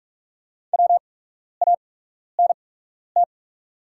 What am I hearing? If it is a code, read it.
Morse code, heard as a keyed tone, about 44 wpm, spaced as for 12 wpm, 690 Hz. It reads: WANT
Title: 8x Character Spacing - Top 500 Words - 45wpm